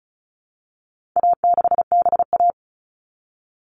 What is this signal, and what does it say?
Morse code, heard as a keyed tone, about 34 wpm, 700 Hz. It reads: A6BA
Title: All Cut Numbers - 4-Digit 35wpm